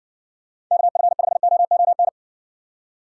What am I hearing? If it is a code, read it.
Morse code, heard as a keyed tone, about 60 wpm, 700 Hz. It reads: KF5OON